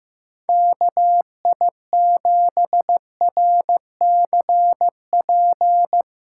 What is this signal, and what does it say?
Morse code, heard as a keyed tone, about 15 wpm, 700 Hz. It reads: KI7RCP
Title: USA Callsign DE Callsign 15wpm